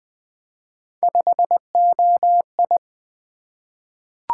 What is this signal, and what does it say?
Morse code, heard as a keyed tone, about 20 wpm, 700 Hz. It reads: 5OI
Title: Three Letter-Number Combinations - 20wpm